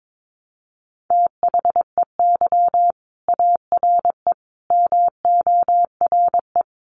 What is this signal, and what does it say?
Morse code, heard as a keyed tone, about 22 wpm, 700 Hz. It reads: THEY ARE MORE
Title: Sets of 5 Words Encore2 22wpm